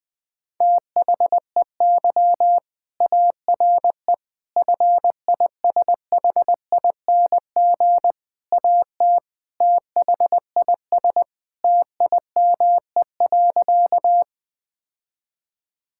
Morse code, heard as a keyed tone, about 20 wpm, 700 Hz. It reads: THEY ARE FISHING AT THIS TIME.
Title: Sentences from Top 1000 Words - Review Part 5 - 20wpm